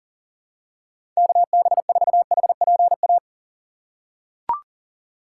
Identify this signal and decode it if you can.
Morse code, heard as a keyed tone, about 37 wpm, 700 Hz. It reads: KB4HPA
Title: USA Call Signs - 6x Character Spacing - 40wpm